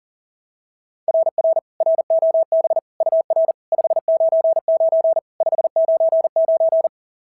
Morse code, heard as a keyed tone, about 42 wpm, 690 Hz. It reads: RR ROB UR 599 599
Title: SOTA QSO Practice 40wpm